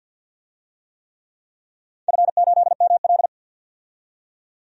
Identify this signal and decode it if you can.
Morse code, heard as a keyed tone, about 47 wpm, 700 Hz. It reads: F8ML
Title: DXCC-100 Call Signs - 4 Characters - 50wpm